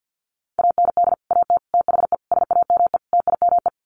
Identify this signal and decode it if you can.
Morse code, heard as a keyed tone, about 50 wpm, 700 Hz. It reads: AND AT THE SAME TIME
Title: Sets of 5 Words 50wpm